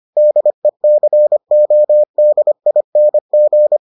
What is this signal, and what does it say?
Morse code, heard as a keyed tone, about 25 wpm, 600 Hz. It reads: DECODING